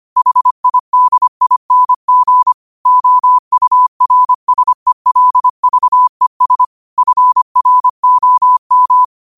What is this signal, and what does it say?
Morse code, heard as a keyed tone, about 25 wpm, 1000 Hz. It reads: HIDING OURSELVES FROM
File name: qotdfiles / 25wpm